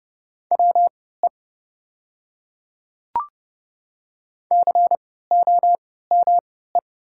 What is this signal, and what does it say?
Morse code, heard as a keyed tone, about 30 wpm, spaced as for 17 wpm, 700 Hz. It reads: WE COME
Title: 3x Character Spacing - Top 100 Words - 30wpm